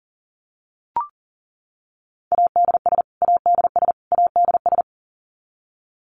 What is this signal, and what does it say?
Morse code, heard as a keyed tone, about 40 wpm, 700 Hz. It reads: ADS ADS ADS
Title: Abbreviations Repeated 3x - 40wpm